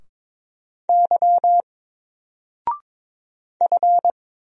Morse code, heard as a keyed tone, about 22 wpm, 700 Hz. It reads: Y F
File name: Single-Letter-Number-22wpm